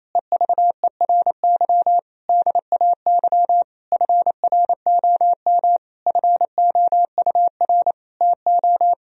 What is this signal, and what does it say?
Morse code, heard as a keyed tone, about 28 wpm, 700 Hz. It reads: EVERY DAY FROM FOUR TO
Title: Sentences from Top 1000 Words - Review Part 2 - 28wpm